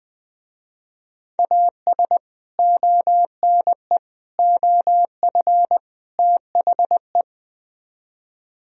Morse code, heard as a keyed tone, about 20 wpm, 700 Hz. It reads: AS ONE OF THE